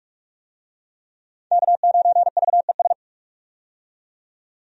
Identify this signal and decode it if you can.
Morse code, heard as a keyed tone, about 42 wpm, 700 Hz. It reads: K9VES
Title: USA Call Signs - 5 Characters - 45wpm